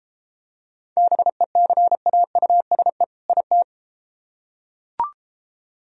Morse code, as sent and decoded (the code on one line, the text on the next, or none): -... -.-. .- ..- ... / .. -
BECAUSE IT